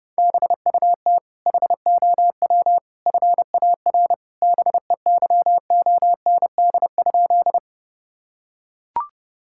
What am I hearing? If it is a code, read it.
Morse code, heard as a keyed tone, about 28 wpm, 700 Hz. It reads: BUT HOW FAR BEYOND?
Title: Sentences from Top 1000 Words - Review Part 6 - 30wpm